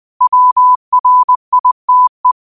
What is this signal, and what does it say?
Morse code, heard as a keyed tone, about 20 wpm, 1000 Hz. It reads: WRITE
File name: qotdfiles / 20wpm